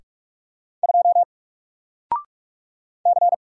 Morse code, heard as a keyed tone, about 46 wpm, 700 Hz. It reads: 2 C
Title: Single Letter-Number 45wpm